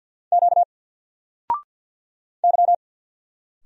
Morse code, heard as a keyed tone, about 50 wpm, 700 Hz. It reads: Q Y
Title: Single Letter-Number 50wpm